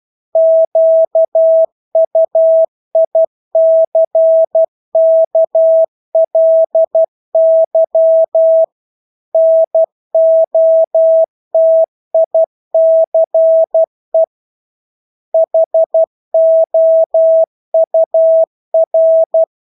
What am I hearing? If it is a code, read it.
Morse code, heard as a keyed tone, about 12 wpm, 650 Hz. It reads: QUICKLY NOTICE HOUR
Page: Engelske ord 12 wpm